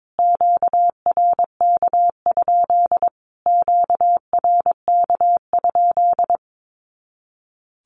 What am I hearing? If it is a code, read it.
Morse code, heard as a keyed tone, about 22 wpm, 700 Hz. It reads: QRK? QRK?
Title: Q-Codes Repeated 3x 22wpm